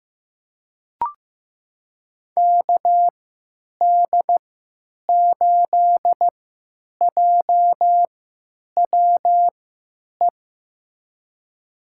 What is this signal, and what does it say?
Morse code, heard as a keyed tone, about 15 wpm, spaced as for 9 wpm, 700 Hz. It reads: KD8JWE
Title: USA Call Signs - 6 Characters - 3x Character Spacing - 15wpm